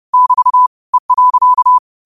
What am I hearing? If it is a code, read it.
Morse code, heard as a keyed tone, about 30 wpm, 1000 Hz. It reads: <BT> E.
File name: qotdfiles / 30wpm